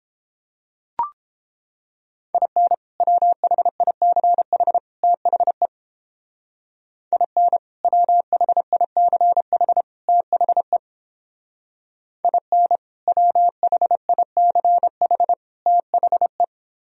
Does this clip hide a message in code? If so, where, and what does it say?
Yes: Morse code from 2.3 to 16.4 s, heard as a keyed tone, about 29 wpm, 700 Hz. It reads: IN WHICH THE IN WHICH THE IN WHICH THE